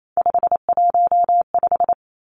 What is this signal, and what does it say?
Morse code, heard as a keyed tone, about 28 wpm, 690 Hz. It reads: 515
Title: Three Challenging Letter-Number Combinations 28wpm